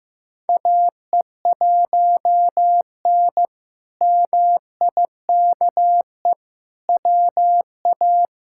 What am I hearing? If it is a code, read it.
Morse code, heard as a keyed tone, about 16 wpm, 700 Hz. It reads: AE1N MIKE WA
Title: K1USN SST QSO Practice - 15wpm